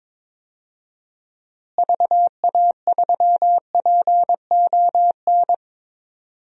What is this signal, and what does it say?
Morse code, heard as a keyed tone, about 22 wpm, 700 Hz. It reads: VA3PON
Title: DXCC-100 Call Signs - 6 Characters - Repeated 3x - 22wpm